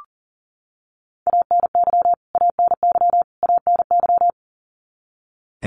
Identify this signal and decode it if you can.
Morse code, heard as a keyed tone, about 40 wpm, 700 Hz. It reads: ANY ANY ANY
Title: Top 300 Words Repeated 40wpm